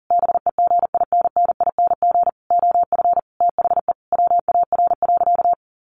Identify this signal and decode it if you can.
Morse code, heard as a keyed tone, about 40 wpm, 700 Hz. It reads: BEGINNING OF THE WAR.
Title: Sentences from Top 1400 Words 40wpm